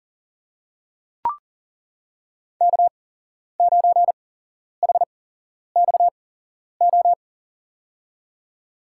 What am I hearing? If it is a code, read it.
Morse code, heard as a keyed tone, about 40 wpm, spaced as for 11 wpm, 700 Hz. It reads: K9HXO
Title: USA Call Signs - 5 Characters - 8x Character Spacing - 40wpm